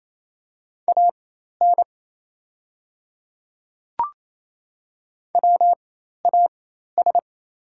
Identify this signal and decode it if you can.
Morse code, heard as a keyed tone, about 27 wpm, spaced as for 13 wpm, 690 Hz. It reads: AN WAS
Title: 4x Character Spacing - Top 100 Words - 28wpm